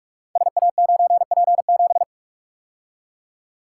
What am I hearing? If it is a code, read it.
Morse code, heard as a keyed tone, about 44 wpm, 690 Hz. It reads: SA9P7
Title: Code Groups - 5 Letter:Numbers - 45wpm